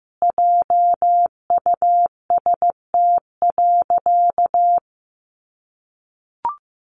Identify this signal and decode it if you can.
Morse code, heard as a keyed tone, about 15 wpm, 700 Hz. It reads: JUST.
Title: Sentences from Top 1400 Words 15wpm